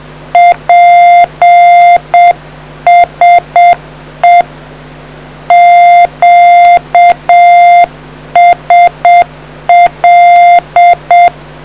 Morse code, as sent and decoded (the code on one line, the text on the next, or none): .--. ... / --.- ... .-..
PSE QSL